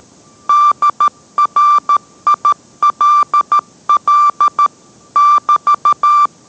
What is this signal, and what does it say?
Morse code, heard as a keyed tone, about 13 wpm, 1180 Hz. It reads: DRILL <BT>